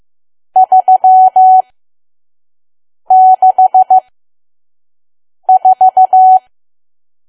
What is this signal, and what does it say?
Morse code, heard as a keyed tone, about 15 wpm, spaced as for 5 wpm, 750 Hz. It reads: 364
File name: Archive / 5 WPM files / PM